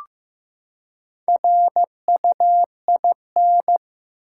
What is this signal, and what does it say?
Morse code, heard as a keyed tone, about 15 wpm, 700 Hz. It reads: RUIN
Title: Four Letter Words - 15wpm